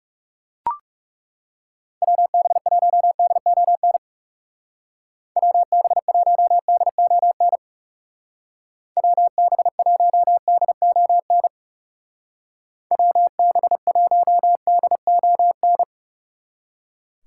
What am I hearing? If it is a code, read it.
Morse code, heard as a keyed tone, about 36 wpm, 700 Hz. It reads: WB1DON WB1DON WB1DON WB1DON
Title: Speed Racing - USA 2x3 Call Signs 30wpm